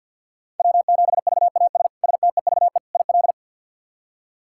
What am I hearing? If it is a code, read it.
Morse code, heard as a keyed tone, about 46 wpm, 700 Hz. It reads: W7VAS STEVE IL
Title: K1USN SST QSO Practice - 50wpm